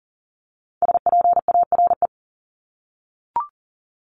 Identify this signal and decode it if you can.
Morse code, heard as a keyed tone, about 38 wpm, 700 Hz. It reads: SPARE
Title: Top 1751-2000 Words 40wpm